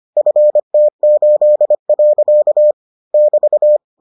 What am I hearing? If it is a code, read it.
Morse code, heard as a keyed tone, about 25 wpm, 600 Hz. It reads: FT8. <BT>